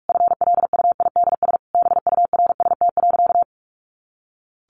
Morse code, heard as a keyed tone, about 44 wpm, 700 Hz. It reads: FLUIDS BURST.
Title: A Sound Of Thunder 45wpm